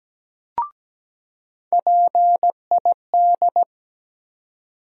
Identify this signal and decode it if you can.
Morse code, heard as a keyed tone, about 17 wpm, 700 Hz. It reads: PID
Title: USA Call Sign Postfixes - 3 Letters 17wpm